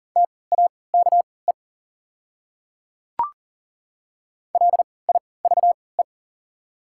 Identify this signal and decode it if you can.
Morse code, heard as a keyed tone, about 40 wpm, spaced as for 23 wpm, 690 Hz. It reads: TAKE LIVE
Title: 3x Character Spacing - Top 300 Words - 40wpm